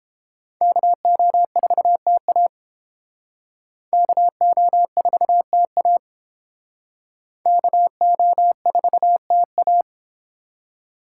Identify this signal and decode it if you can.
Morse code, heard as a keyed tone, about 31 wpm, 700 Hz. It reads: KO4TA KO4TA KO4TA